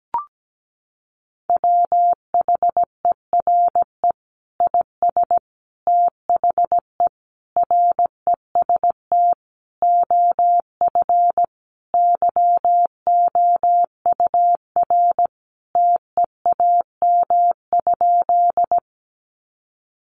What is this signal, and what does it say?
Morse code, heard as a keyed tone, about 17 wpm, 700 Hz. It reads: WHERE IS THE REST OF YOUR TEAM?
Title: Sentences from Top 1500 Words 17wpm